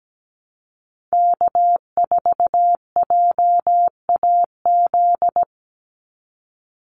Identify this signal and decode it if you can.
Morse code, heard as a keyed tone, about 17 wpm, 700 Hz. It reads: K4JAZ
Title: Call Signs Repeated 3x - 17wpm